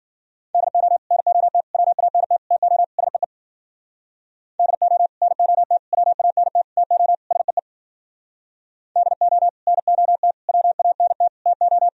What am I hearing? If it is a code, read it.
Morse code, heard as a keyed tone, about 52 wpm, 700 Hz. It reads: DO NOT WANT TO SEE DO NOT WANT TO SEE DO NOT WANT TO